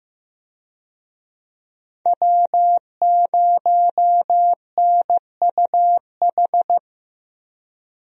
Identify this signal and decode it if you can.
Morse code, heard as a keyed tone, about 15 wpm, 700 Hz. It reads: W0NUH
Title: USA Call Signs - 5 Characters - Repeated 2x - 15wpm